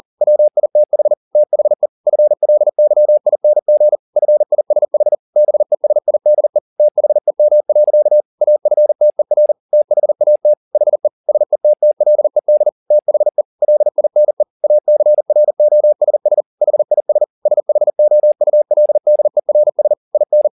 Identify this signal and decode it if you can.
Morse code, heard as a keyed tone, about 40 wpm, 600 Hz. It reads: WITH THE FLYING FISH BESIDE THEM. AFTER THAT HE SETTLED THE LINE ACROSS HIS SHOULDERS IN
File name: the old man and the sea 2 40wpm